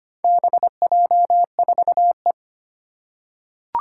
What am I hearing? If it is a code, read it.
Morse code, heard as a keyed tone, about 25 wpm, 700 Hz. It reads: BJ4E
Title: Four Challenging Letter-Number Combinations 25wpm